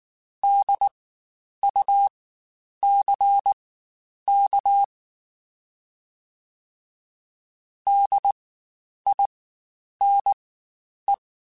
Morse code, heard as a keyed tone, about 19 wpm, spaced as for 9 wpm, 800 Hz. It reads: DUCK DINE
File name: Spaced Alphas / Spaced Alphas mp3s / Ds sp @18